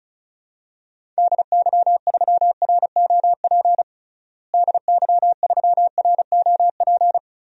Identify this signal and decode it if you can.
Morse code, heard as a keyed tone, about 34 wpm, 700 Hz. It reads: DY3ROP DY3ROP
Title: DXCC-100 Call Signs - 6 Characters - Repeated 2x - 35wpm